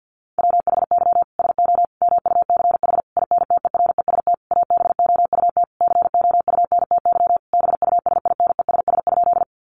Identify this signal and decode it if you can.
Morse code, heard as a keyed tone, about 50 wpm, 700 Hz. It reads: WHY SO MUCH INTEREST ABOUT COUNTY BUSINESS?
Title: Sentences from Top 1200 Words 50wpm